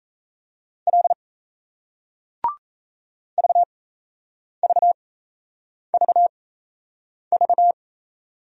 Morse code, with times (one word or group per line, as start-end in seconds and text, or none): 0.9-1.1 L
3.4-7.7 VVVV